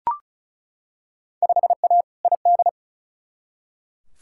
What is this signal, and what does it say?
Morse code, heard as a keyed tone, about 34 wpm, 700 Hz. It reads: 5A ID
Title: Field Day Exchanges - 35wpm